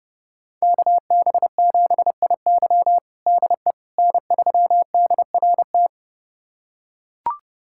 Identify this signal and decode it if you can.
Morse code, heard as a keyed tone, about 30 wpm, 700 Hz. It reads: KB7IY DE N3DRT
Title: USA Callsign DE Callsign - 5 Characters - 30wpm